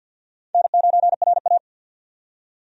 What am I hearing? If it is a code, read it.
Morse code, heard as a keyed tone, about 51 wpm, 700 Hz. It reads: N9RA
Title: USA Call Signs - 4 Characters - 50wpm